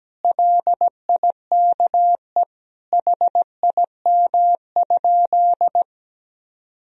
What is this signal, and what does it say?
Morse code, heard as a keyed tone, about 17 wpm, 700 Hz. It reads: LIKE HIM?
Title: Sentences from Top 100 Words and Abbreviations 17wpm